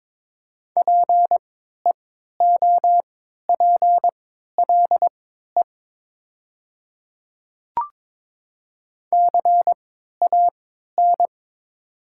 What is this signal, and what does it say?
Morse code, heard as a keyed tone, about 22 wpm, spaced as for 13 wpm, 700 Hz. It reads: PEOPLE CAN
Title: 3x Character Spacing - Top 100 Words - 22wpm